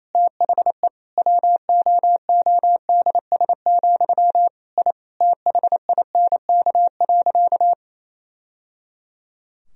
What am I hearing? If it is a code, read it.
Morse code, heard as a keyed tone, about 28 wpm, 690 Hz. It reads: THE WOODS, I THINK.
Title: Sentences from Top 1000 Words - Review Part 2 - 28wpm